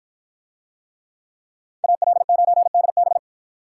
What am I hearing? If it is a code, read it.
Morse code, heard as a keyed tone, about 53 wpm, 700 Hz. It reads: AL9DB